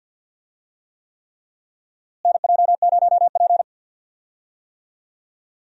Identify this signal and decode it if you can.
Morse code, heard as a keyed tone, about 50 wpm, 700 Hz. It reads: NJ0P
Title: USA Call Signs - 4 Characters - Repeated 3x - 50wpm